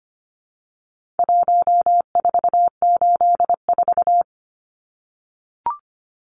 Morse code, heard as a keyed tone, about 25 wpm, 700 Hz. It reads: 1484
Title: Numbers - Contest Style - 4-Digit 25wpm